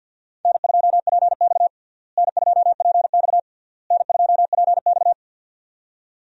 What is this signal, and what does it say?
Morse code, heard as a keyed tone, about 53 wpm, 700 Hz. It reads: N2PX N2PX N2PX
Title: USA Call Signs - 4 Characters - Repeated 3x - 50wpm